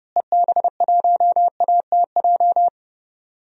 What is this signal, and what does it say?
Morse code, heard as a keyed tone, about 31 wpm, 700 Hz. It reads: EB1ATJ
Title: DXCC-100 Call Signs - 6 Characters - Repeated 2x - 30wpm